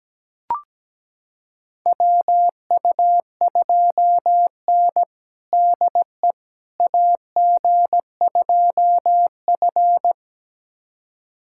Morse code, heard as a keyed tone, about 17 wpm, 700 Hz. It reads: WU2N DE AG2F
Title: USA Callsign DE Callsign - 4 Characters - 17wpm